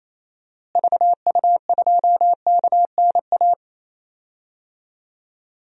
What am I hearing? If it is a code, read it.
Morse code, heard as a keyed tone, about 28 wpm, 700 Hz. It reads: VU2KNA
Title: DXCC-100 Call Signs - 6 Characters - Repeated 2x - 28wpm